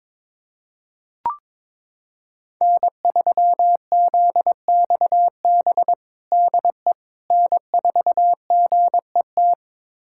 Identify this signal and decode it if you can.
Morse code, heard as a keyed tone, about 21 wpm, 700 Hz. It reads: N3ZXB DE N4GET
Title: USA Callsign DE Callsign - 5 Characters - 22wpm